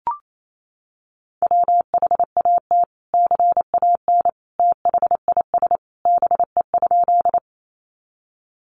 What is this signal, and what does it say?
Morse code, heard as a keyed tone, about 28 wpm, 700 Hz. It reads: WHAT CAN THIS BE?
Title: Sentences from Top 100 Words 28wpm